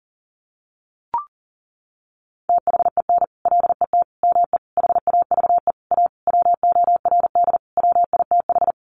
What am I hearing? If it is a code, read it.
Morse code, heard as a keyed tone, about 40 wpm, 700 Hz. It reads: THEN LET ME HAVE A WORD WITH
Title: Sentences from Top 300 Words 40wpm